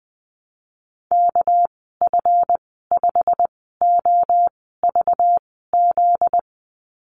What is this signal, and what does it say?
Morse code, heard as a keyed tone, about 20 wpm, spaced as for 14 wpm, 700 Hz. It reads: KF5OVZ
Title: 245 Lesson 39g - Letter z - Call Signs 20wpm